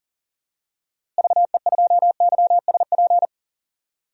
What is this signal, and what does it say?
Morse code, heard as a keyed tone, about 37 wpm, 700 Hz. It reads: VE2YSP
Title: DXCC-100 Call Signs - 6 Characters - Repeated 2x - 40wpm